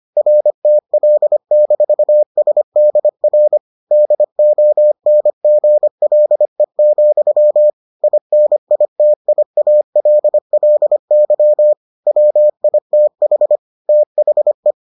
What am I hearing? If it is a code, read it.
Morse code, heard as a keyed tone, about 25 wpm, 600 Hz. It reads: RTL-SDR DONGLE, INITIALLY WITH THE